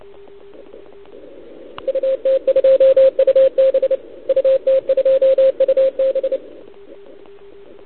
Morse code, heard as a keyed tone, about 30 wpm, 530 Hz, with radio noise. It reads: UT2UB UT2UB
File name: ARCHIV / OKDX2019 / OKOM40 KX3